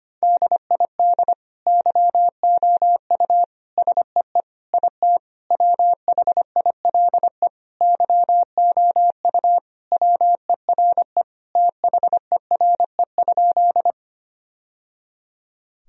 Morse code, heard as a keyed tone, about 25 wpm, 700 Hz. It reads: DID YOU SEE IT WHILE YOU WERE THERE?
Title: Sentences from Top 1000 Words - Review Part 1 - 25wpm